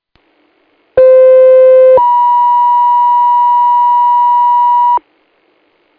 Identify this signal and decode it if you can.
Two-tone page, heard as a one-second tone, then a three-second tone, about 520 then 970 Hz. It is heard from 1.0 to 5.0 s.